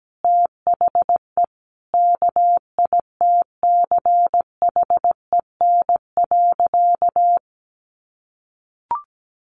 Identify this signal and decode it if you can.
Morse code, heard as a keyed tone, about 17 wpm, 700 Hz. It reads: THE KITCHEN.
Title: Easy Phrases 17wpm